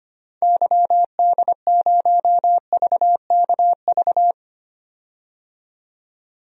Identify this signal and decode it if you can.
Morse code, heard as a keyed tone, about 25 wpm, 700 Hz. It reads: YD0VKV
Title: DXCC-100 Call Signs - 6 Characters - 25wpm